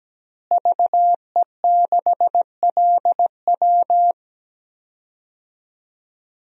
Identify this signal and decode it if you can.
Morse code, heard as a keyed tone, about 17 wpm, 700 Hz. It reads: VE6LW
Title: DXCC-100 Call Signs - 5 Characters - Repeated 3x - 17wpm